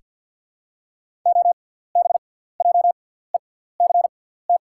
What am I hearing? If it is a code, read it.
Morse code, heard as a keyed tone, about 48 wpm, spaced as for 17 wpm, 690 Hz. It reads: OBJECT
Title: 6x Character Spacing - Top 500 Words - 50wpm